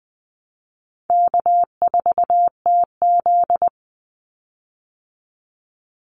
Morse code, heard as a keyed tone, about 20 wpm, 700 Hz. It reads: K4TZ
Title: Call Signs Repeated 3x - 20wpm